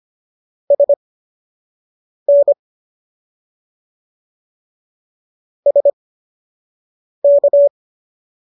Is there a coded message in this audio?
Yes: Morse code from 0.7 to 7.7 s, heard as a keyed tone, about 25 wpm, spaced as for 6 wpm, 570 Hz. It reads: SN SK